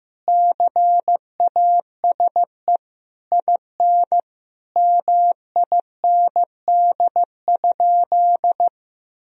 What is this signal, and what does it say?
Morse code, heard as a keyed tone, about 15 wpm, 700 Hz. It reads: CASE IN MIND?
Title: Sentences from Top 1000 Words - Review Part 4 - 15wpm